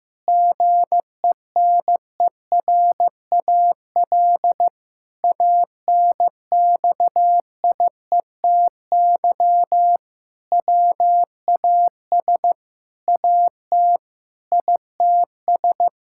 Morse code, heard as a keyed tone, about 16 wpm, 700 Hz. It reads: GENERAL ANXIETY WAS AT ITS
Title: Sentences from Top 1600 Words 15wpm